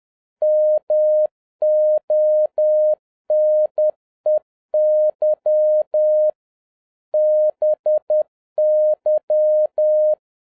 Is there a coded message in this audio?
Yes: Morse code from 0.4 to 10.1 s, heard as a keyed tone, about 10 wpm, 610 Hz. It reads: MONEY BY